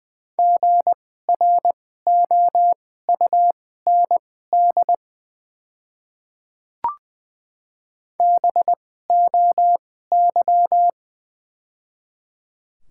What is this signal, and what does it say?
Morse code, heard as a keyed tone, about 20 wpm, spaced as for 15 wpm, 700 Hz. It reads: GROUND BOY
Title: 2x Character Spacing - Top 500 Words - 20wpm